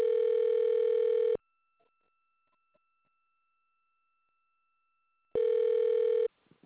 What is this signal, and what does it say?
Ringback tone (North America), 440 + 480 Hz.